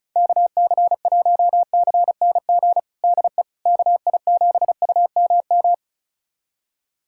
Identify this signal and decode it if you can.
Morse code, heard as a keyed tone, about 34 wpm, 700 Hz. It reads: KC1CNG DE KI7UMM
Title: USA Callsign DE Callsign 35wpm